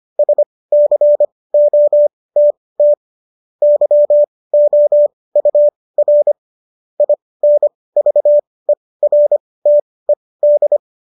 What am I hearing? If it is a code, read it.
Morse code, heard as a keyed tone, about 25 wpm, spaced as for 18 wpm, 590 Hz. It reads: SCOTT YOUR INVERTED
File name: qso223 18